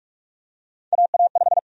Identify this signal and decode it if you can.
Morse code, heard as a keyed tone, about 44 wpm, 700 Hz. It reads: AA5
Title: USA Call Sign Prefixes - Repeated 2x - 45wpm